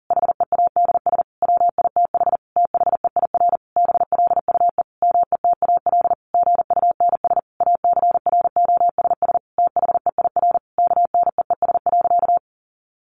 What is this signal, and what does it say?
Morse code, heard as a keyed tone, about 40 wpm, 700 Hz. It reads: HEADS WITH THEIR BLUE METAL GUNS ACROSS THEIR KNEES.
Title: A Sound Of Thunder 40wpm